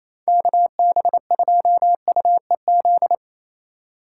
Morse code, heard as a keyed tone, about 27 wpm, 700 Hz. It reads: KB2UEZ
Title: USA Call Signs - 6 Characters - 3x Character Spacing - 28wpm